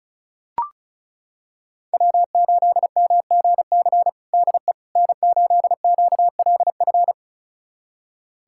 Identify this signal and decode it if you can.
Morse code, heard as a keyed tone, about 34 wpm, 700 Hz. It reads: W8MGC DE N8QLF
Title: USA Callsign DE Callsign - 5 Characters - 35wpm